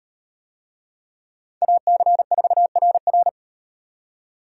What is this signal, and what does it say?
Morse code, heard as a keyed tone, about 38 wpm, 700 Hz. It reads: AC4RR